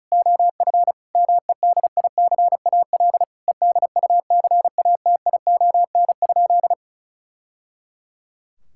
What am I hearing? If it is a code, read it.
Morse code, heard as a keyed tone, about 34 wpm, 700 Hz. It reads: OF MEDICAL EDUCATION?
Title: Sentences from Top 1000 Words - Review Part 6 - 35wpm